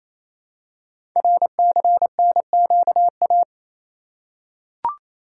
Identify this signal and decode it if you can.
Morse code, heard as a keyed tone, about 27 wpm, 700 Hz. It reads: RCNQA
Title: Code Groups - 5 Letters - 28wpm